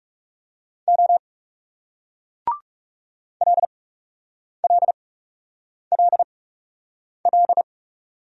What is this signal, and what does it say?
Morse code, heard as a keyed tone, about 42 wpm, spaced as for 8 wpm, 700 Hz. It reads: O LLLL